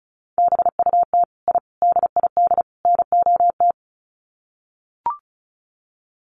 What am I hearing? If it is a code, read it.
Morse code, heard as a keyed tone, about 34 wpm, 700 Hz. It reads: BUT I DID NOT
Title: Sets of 4 Words 35wpm